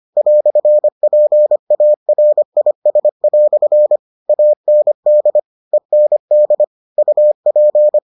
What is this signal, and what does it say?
Morse code, heard as a keyed tone, about 25 wpm, 600 Hz. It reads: "PARIS" AND END UP